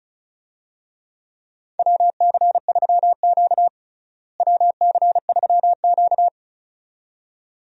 Morse code, heard as a keyed tone, about 34 wpm, 700 Hz. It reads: WC3Q WC3Q
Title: USA Call Signs - 4 Characters - Repeated 2x - 35wpm